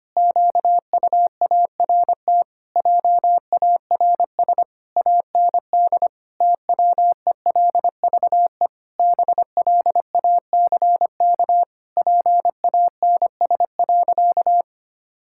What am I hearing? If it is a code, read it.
Morse code, heard as a keyed tone, about 25 wpm, 700 Hz. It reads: QUART JARS AND TWELVE BLACK PANS.
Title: Pangram Challenge v2 - 25wpm